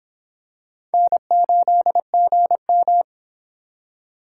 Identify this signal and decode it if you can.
Morse code, heard as a keyed tone, about 26 wpm, 700 Hz. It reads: N8GM